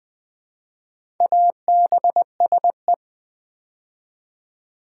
Morse code, heard as a keyed tone, about 20 wpm, 700 Hz. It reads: ABSE
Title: Build A Word - 2 Letters At A Time - 6-Letter Words - 20wpm